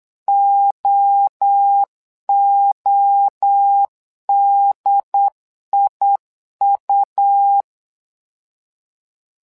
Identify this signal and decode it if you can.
Morse code, heard as a keyed tone, about 8 wpm, 800 Hz. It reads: OODIU
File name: LEZIONI / 3 MWRO / 3 MWRO 8wpm 2606